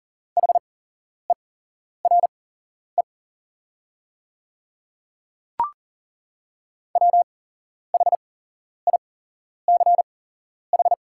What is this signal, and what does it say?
Morse code, heard as a keyed tone, about 40 wpm, spaced as for 11 wpm, 700 Hz. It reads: HERE WHICH
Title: 8x Character Spacing - Top 300 Words - 40wpm